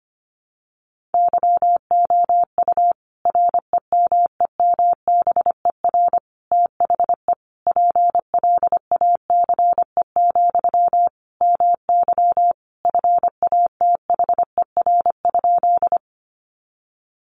Morse code, heard as a keyed tone, about 25 wpm, 700 Hz. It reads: YOU REMEMBER THE PLACE, MY FATHER?
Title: Sentences from Top 400 Words 25wpm